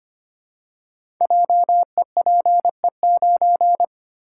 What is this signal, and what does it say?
Morse code, heard as a keyed tone, about 25 wpm, 700 Hz. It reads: JEPE9
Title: Code Groups - 5 Letter:Numbers - 25wpm